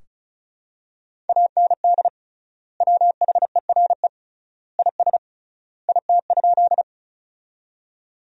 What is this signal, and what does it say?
Morse code, heard as a keyed tone, about 34 wpm, 700 Hz. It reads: AND WHERE IS IT?
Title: 3x Word-Spacing - Sentences from Top 200 Words - 35wpm